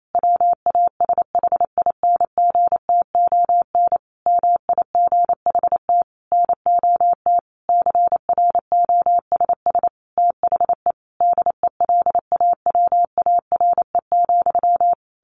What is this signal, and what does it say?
Morse code, heard as a keyed tone, about 28 wpm, 700 Hz. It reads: WASHINGTON MIGHT NOT CROSS THE DELAWARE,
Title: A Sound Of Thunder 28wpm